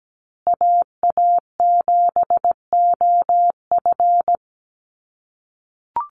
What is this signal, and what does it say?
Morse code, heard as a keyed tone, about 17 wpm, 700 Hz. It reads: AA7OF
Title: USA Call Signs Repeated 3x 17wpm